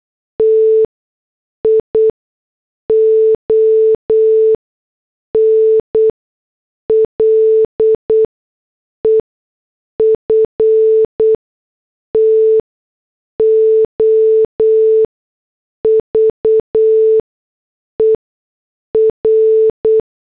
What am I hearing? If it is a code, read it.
Morse code, heard as a keyed tone, about 8 wpm, spaced as for 6 wpm, 440 Hz. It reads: TION LEFTOVER
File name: morse / headlines